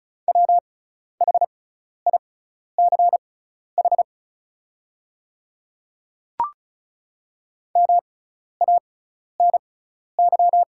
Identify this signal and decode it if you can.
Morse code, heard as a keyed tone, about 34 wpm, spaced as for 12 wpm, 700 Hz. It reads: WHICH MANY
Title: 6x Character Spacing - Top 200 Words - 35wpm